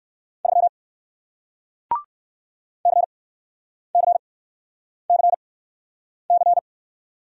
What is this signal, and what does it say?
Morse code, heard as a keyed tone, about 60 wpm, spaced as for 9 wpm, 700 Hz. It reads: 3 CCCC